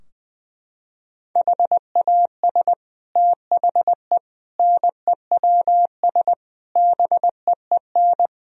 Morse code, heard as a keyed tone, about 21 wpm, 700 Hz. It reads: HAS THE NEWS BEEN
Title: Sentences from Top 1000 Words - Review Part 3 - 20wpm